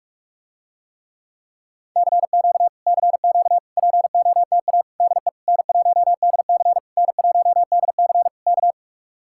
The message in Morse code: CQ CQ POTA DE N1DC N1DC K

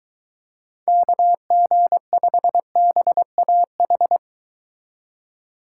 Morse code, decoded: KG5BAH

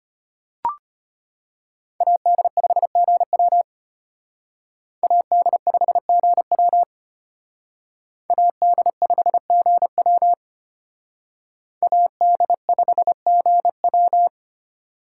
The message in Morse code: AD5GW AD5GW AD5GW AD5GW

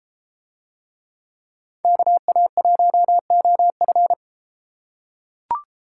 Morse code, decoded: KA1OF

